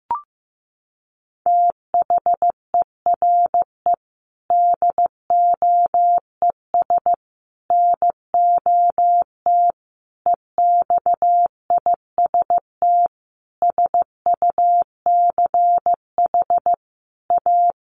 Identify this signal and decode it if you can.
Morse code, heard as a keyed tone, about 15 wpm, 700 Hz. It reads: THERE DOES NOT EXIST SUCH A
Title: Sentences from Top 1400 Words 15wpm